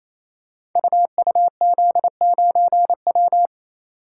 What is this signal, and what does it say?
Morse code, heard as a keyed tone, about 28 wpm, 700 Hz. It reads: UUZ9W
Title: Code Groups - 5 Letter:Numbers - 28wpm